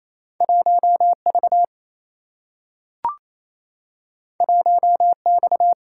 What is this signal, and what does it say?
Morse code, heard as a keyed tone, about 28 wpm, 700 Hz. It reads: 1V 1X
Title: Single-Letter Single-Number 28wpm